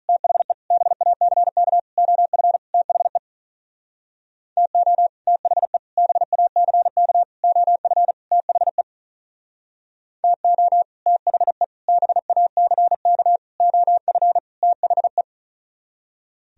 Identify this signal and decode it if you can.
Morse code, heard as a keyed tone, about 44 wpm, 700 Hz. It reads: THE BACK OF THE TO THE BACK OF THE TO THE BACK OF THE